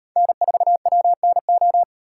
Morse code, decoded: N4WNO

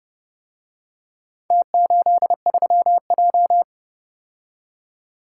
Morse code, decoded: T83J

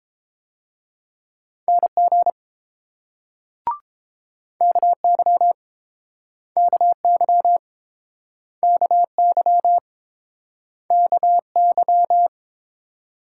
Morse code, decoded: NG KY KY KY KY

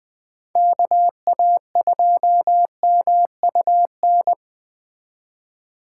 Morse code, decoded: KA2MUN

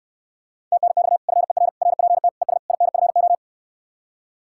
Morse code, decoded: TNX FER RPT ES INFO

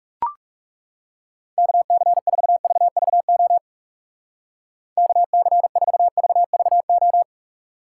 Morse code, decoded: KC4VVO KC4VVO

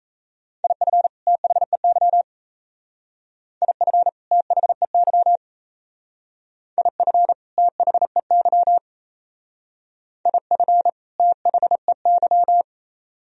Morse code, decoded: IF THEY IF THEY IF THEY IF THEY

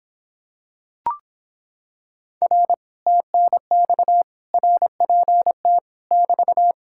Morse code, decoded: R TNX RPT <BT>

